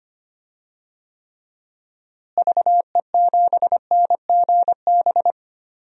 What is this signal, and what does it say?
Morse code, heard as a keyed tone, about 25 wpm, 700 Hz. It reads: VE7NGB
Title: DXCC-100 Call Signs - 6 Characters - Repeated 2x - 25wpm